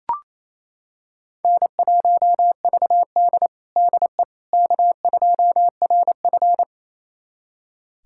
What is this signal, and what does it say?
Morse code, heard as a keyed tone, about 28 wpm, 700 Hz. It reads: N1VD DE K2RF
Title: USA Callsign DE Callsign - 4 Characters - 28wpm